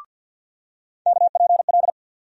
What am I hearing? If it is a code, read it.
Morse code, heard as a keyed tone, about 53 wpm, 700 Hz. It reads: KPL
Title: USA Call Sign Postfixes - 3 Letters 50wpm